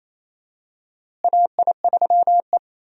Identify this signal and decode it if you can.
Morse code, heard as a keyed tone, about 28 wpm, 700 Hz. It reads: AI3E
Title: USA Call Signs - 4 Characters - Repeated 3x - 28wpm